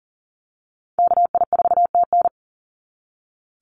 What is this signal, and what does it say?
Morse code, heard as a keyed tone, about 40 wpm, 700 Hz. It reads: KI4TN
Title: NA SOTA Activator Callsigns 40wpm